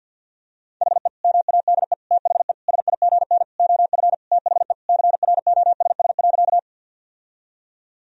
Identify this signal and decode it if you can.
Morse code, heard as a keyed tone, about 52 wpm, 700 Hz. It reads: HE MADE THE SIGN OF THE CROSS.
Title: Sentences from Top 1000 Words - Review Part 1 - 50wpm